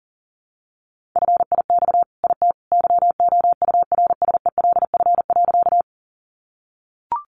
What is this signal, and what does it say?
Morse code, heard as a keyed tone, about 40 wpm, 700 Hz. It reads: FIX IT YOURSELF.
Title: Sentences from Top 1400 Words 40wpm